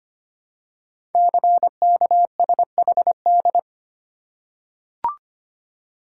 Morse code, heard as a keyed tone, about 25 wpm, 700 Hz. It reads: CKSHD
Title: Code Groups - 5 Letters - 25wpm